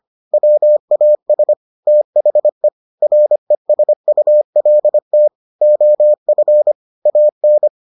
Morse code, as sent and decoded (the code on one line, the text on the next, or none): .-- .- ... / - .... / .-. ... ..- .-.. - / --- ..-. / .- -.
WAS THE RESULT OF AN